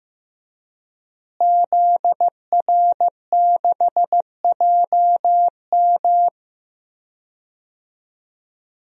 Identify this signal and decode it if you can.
Morse code, heard as a keyed tone, about 15 wpm, 700 Hz. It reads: ZR6JM
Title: DXCC-100 Call Signs - 5 Characters - Repeated 2x - 15wpm